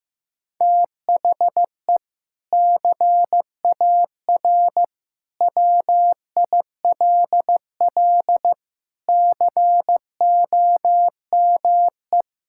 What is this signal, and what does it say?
Morse code, heard as a keyed tone, about 15 wpm, 700 Hz. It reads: THE CAR WILL COME
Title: Sentences from Top 1600 Words 15wpm